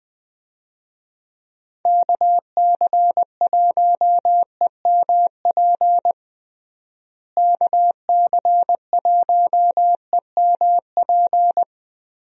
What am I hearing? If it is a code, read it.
Morse code, heard as a keyed tone, about 20 wpm, 700 Hz. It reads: KC1EMP KC1EMP
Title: USA Call Signs - 6 Characters - Repeated 2x - 20wpm